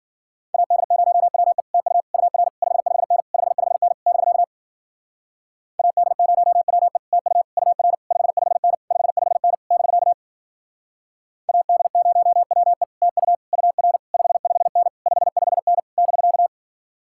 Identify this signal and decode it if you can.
Morse code, heard as a keyed tone, about 52 wpm, 700 Hz. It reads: AD0WE TU UR 55N 55N <BK> AD0WE TU UR 55N 55N <BK> AD0WE TU UR 55N 55N <BK>